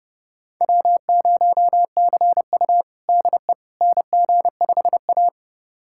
Morse code, heard as a keyed tone, about 31 wpm, 700 Hz. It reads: W0CU DE NG5A
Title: USA Callsign DE Callsign - 4 Characters - 30wpm